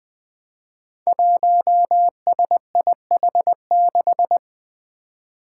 Morse code, heard as a keyed tone, about 20 wpm, 700 Hz. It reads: 1SIH6
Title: Five Challenging Letter-Number Combinations 20wpm